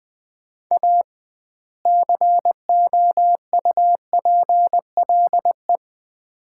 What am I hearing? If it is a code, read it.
Morse code, heard as a keyed tone, about 20 wpm, 700 Hz. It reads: A COUPLE
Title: 2x Word-Spacing - Sets of 2 Words - 20wpm